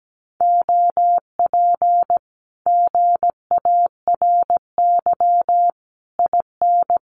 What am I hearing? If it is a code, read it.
Morse code, heard as a keyed tone, about 17 wpm, 700 Hz. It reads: OP GARY IN
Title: Common QSO Phrases Encore 17wpm